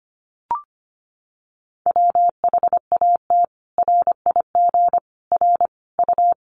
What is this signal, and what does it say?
Morse code, heard as a keyed tone, about 25 wpm, 700 Hz. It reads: WHAT RIG R U
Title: Common QSO Phrases 25wpm